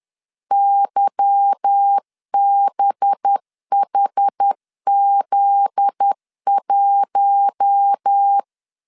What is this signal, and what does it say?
Morse code, heard as a keyed tone, about 10 wpm, 800 Hz. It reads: YBHZ1
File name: LEZIONI / 8 YJ45 / 8 YJ4 10wpm 2633